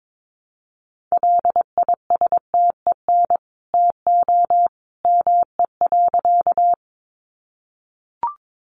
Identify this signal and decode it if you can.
Morse code, heard as a keyed tone, about 22 wpm, 700 Hz. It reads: LISTEN TO ME.
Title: Sentences from Top 400 Words 22wpm